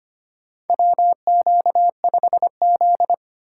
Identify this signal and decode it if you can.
Morse code, heard as a keyed tone, about 25 wpm, 700 Hz. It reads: WQ5Z
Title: USA Call Signs - 4 Characters - 6x Character Spacing - 25wpm